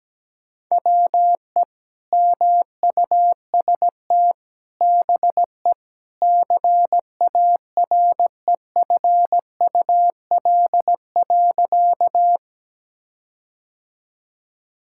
Morse code, heard as a keyed tone, about 17 wpm, 700 Hz. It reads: WE MUST BE CAREFUL.
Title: Sentences from Top 1600 Words 17wpm